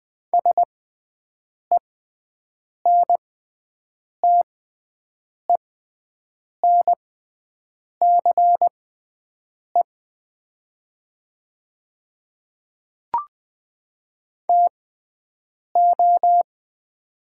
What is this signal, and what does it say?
Morse code, heard as a keyed tone, about 20 wpm, spaced as for 7 wpm, 700 Hz. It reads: SENTENCE TO
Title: 6x Character Spacing - Top 200 Words - 20wpm